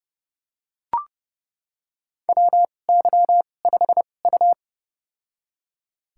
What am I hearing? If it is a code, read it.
Morse code, heard as a keyed tone, about 32 wpm, spaced as for 22 wpm, 700 Hz. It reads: WY5U
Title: USA Call Signs - 4 Characters - 2x Character Spacing - 30wpm